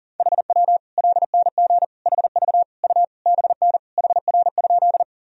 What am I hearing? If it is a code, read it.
Morse code, heard as a keyed tone, about 40 wpm, 700 Hz. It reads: HW LNG HV U BN HR?
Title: Sentences from Top 200 Words and Abbreviations 40wpm